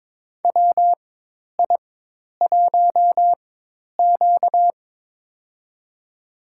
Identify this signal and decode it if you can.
Morse code, heard as a keyed tone, about 22 wpm, spaced as for 10 wpm, 700 Hz. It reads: WI1Q
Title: USA Call Signs - 4 Characters - 4x Character Spacing - 22wpm